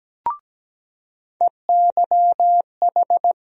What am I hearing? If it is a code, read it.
Morse code, heard as a keyed tone, about 17 wpm, 700 Hz. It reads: EYH
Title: USA Call Sign Postfixes - 3 Letters 17wpm